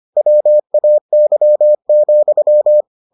Morse code, heard as a keyed tone, about 25 wpm, 600 Hz. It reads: WAY,